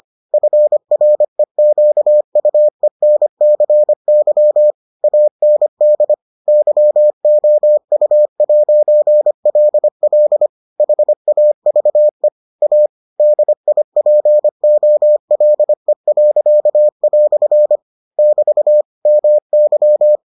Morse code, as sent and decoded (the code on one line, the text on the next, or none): ..-. .-. --.- ..- . -. -.-. -.-- / .- -. -.. / -.-- --- ..- .----. .-.. .-.. / .... .- ...- . / .- / -.. .. .--. --- .-.. .-.-.- .-..-. / -...- / -- -.--
FREQUENCY AND YOU'LL HAVE A DIPOLE." <BT> MY